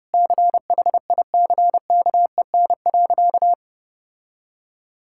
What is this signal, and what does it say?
Morse code, heard as a keyed tone, about 30 wpm, 700 Hz. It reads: CHICKEN.
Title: Sentences from Top 1000 Words - Review Part 5 - 30wpm